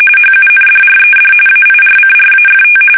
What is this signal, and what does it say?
RTTY, about 45.45 baud. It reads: HUNDRED AIRCRAFT
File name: fsk850 / battleshipnj